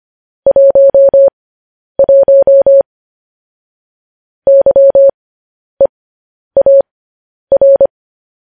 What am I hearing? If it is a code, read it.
Morse code, heard as a keyed tone, about 25 wpm, spaced as for 10 wpm, 550 Hz. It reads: 11 YEAR
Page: News Headlines in Morse Code at 10 WPM